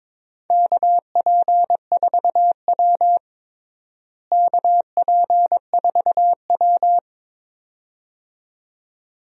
Morse code, heard as a keyed tone, about 22 wpm, 700 Hz. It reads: KP4W KP4W
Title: DXCC-100 Call Signs - 4 Characters - Repeated 2x - 22wpm